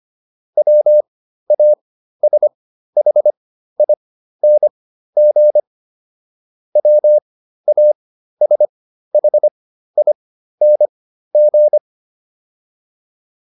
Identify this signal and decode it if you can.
Morse code, heard as a keyed tone, about 25 wpm, spaced as for 13 wpm, 600 Hz. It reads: WASHING WASHING